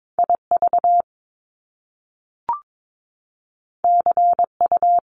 Morse code, heard as a keyed tone, about 22 wpm, 700 Hz. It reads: IV CU
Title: Top Two Letter Combinations in Words 22wpm